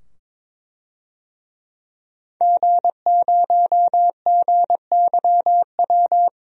Morse code, heard as a keyed tone, about 21 wpm, 700 Hz. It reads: G0GYW
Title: DXCC-100 Call Signs - 5 Characters - 22wpm